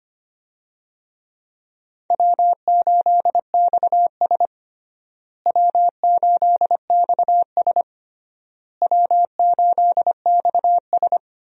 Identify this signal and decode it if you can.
Morse code, heard as a keyed tone, about 25 wpm, 700 Hz. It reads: W8XS W8XS W8XS
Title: USA Call Signs - 4 Characters - Repeated 3x - 25wpm